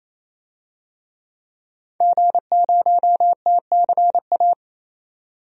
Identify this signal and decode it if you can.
Morse code, heard as a keyed tone, about 28 wpm, 700 Hz. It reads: G0TCA
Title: DXCC-100 Call Signs - 5 Characters - 28wpm